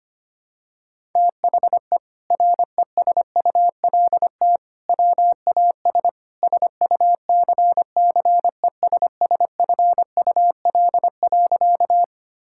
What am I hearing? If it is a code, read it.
Morse code, heard as a keyed tone, about 25 wpm, 700 Hz. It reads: THE RESULT WAS SUCCESSFUL.
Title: Sentences from Top 1000 Words - Review Part 6 - 25wpm